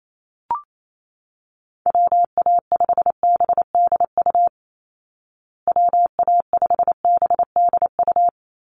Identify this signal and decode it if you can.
Morse code, heard as a keyed tone, about 28 wpm, 700 Hz. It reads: WA5BDU WA5BDU
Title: Call Signs Repeated 3x - 28wpm